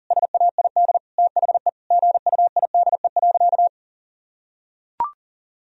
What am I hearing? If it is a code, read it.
Morse code, heard as a keyed tone, about 40 wpm, 700 Hz. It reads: SAID THE GUIDE.
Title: Sentences from Top 1000 Words - Review Part 3 - 40wpm